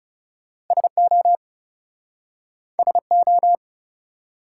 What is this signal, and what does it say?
Morse code, heard as a keyed tone, about 38 wpm, 700 Hz. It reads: SO SO